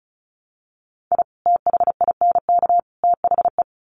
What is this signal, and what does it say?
Morse code, heard as a keyed tone, about 34 wpm, 700 Hz. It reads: I THINK THE
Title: Sets of 3 Words 35wpm